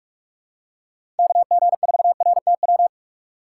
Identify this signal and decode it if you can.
Morse code, heard as a keyed tone, about 46 wpm, 700 Hz. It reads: KG4RTW
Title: USA Call Signs - 6 Characters - 2x Character Spacing - 45wpm